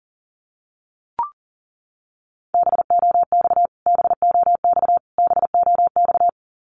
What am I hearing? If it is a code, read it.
Morse code, heard as a keyed tone, about 40 wpm, 700 Hz. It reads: BOX BOX BOX
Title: Top 500 Words Repeated 40wpm